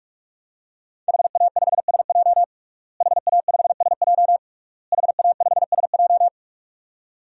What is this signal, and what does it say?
Morse code, heard as a keyed tone, about 46 wpm, 700 Hz. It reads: HA5SJ HA5SJ HA5SJ
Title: DXCC-100 Call Signs - 5 Characters - Repeated 3x - 45wpm